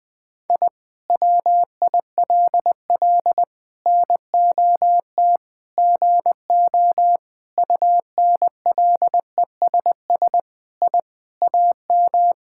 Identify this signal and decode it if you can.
Morse code, heard as a keyed tone, about 21 wpm, 700 Hz. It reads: I WILL NOT GO UNLESS I AM
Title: Sentences from Top 1000 Words - Review Part 6 - 20wpm